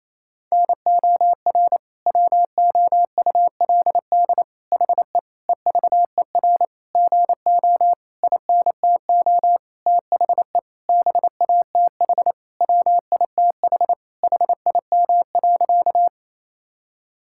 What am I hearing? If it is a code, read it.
Morse code, heard as a keyed tone, about 28 wpm, 700 Hz. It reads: NOR WOULD HE EVER GO INTO THE BATH WITH HIM.
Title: Sentences from Top 1600 Words 28wpm